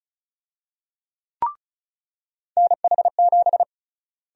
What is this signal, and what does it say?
Morse code, heard as a keyed tone, about 36 wpm, 700 Hz. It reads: NH7
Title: USA Call Sign Prefixes - Repeated 2x - 35wpm